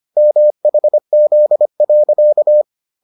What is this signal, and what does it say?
Morse code, heard as a keyed tone, about 25 wpm, 600 Hz. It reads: MHZ.